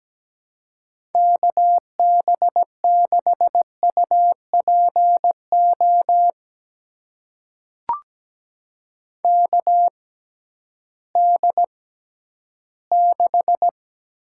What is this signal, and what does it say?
Morse code, heard as a keyed tone, about 17 wpm, spaced as for 11 wpm, 700 Hz. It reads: KB6UPO KD6
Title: USA Call Signs - 6 Characters - 6x Character Spacing - 17wpm